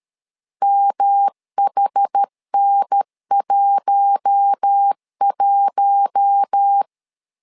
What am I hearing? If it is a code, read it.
Morse code, heard as a keyed tone, about 13 wpm, 800 Hz. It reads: MHN11
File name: LEZIONI / 4 Q1 12wpm 2028